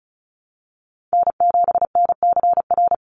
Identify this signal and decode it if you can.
Morse code, heard as a keyed tone, about 34 wpm, 700 Hz. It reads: N7NCR
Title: USA Call Signs 35wpm